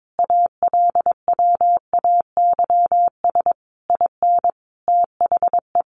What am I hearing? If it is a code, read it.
Morse code, heard as a keyed tone, about 22 wpm, 700 Hz. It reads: ALWAYS IN THE
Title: Sentences from Top 1400 Words 22wpm